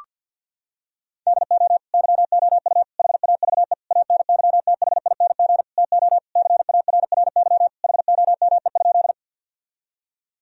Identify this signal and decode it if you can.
Morse code, heard as a keyed tone, about 50 wpm, 700 Hz. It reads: DO YOU HAVE ANYTHING TO CARRY HOME?
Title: Sentences from Top 1000 Words - Review Part 1 - 50wpm